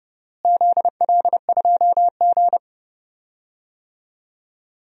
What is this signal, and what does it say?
Morse code, heard as a keyed tone, about 30 wpm, 700 Hz. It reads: ZL2G
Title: DXCC-100 Call Signs - 4 Characters - Repeated 2x - 30wpm